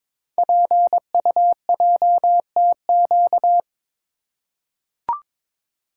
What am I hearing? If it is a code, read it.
Morse code, heard as a keyed tone, about 22 wpm, 700 Hz. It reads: PUJTQ
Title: Code Groups - 5 Letter:Numbers - 22wpm